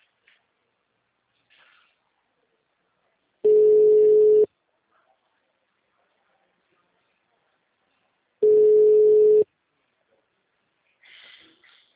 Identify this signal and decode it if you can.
Ringback tone (Europe, 425 Hz).